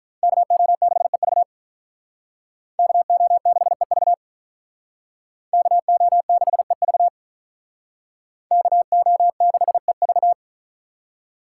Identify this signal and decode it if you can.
Morse code, heard as a keyed tone, about 44 wpm, 700 Hz. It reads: KO6EV KO6EV KO6EV KO6EV